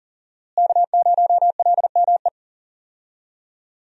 Morse code, heard as a keyed tone, about 37 wpm, 700 Hz. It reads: K0LME
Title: USA Call Signs - 5 Characters - Repeated 2x - 40wpm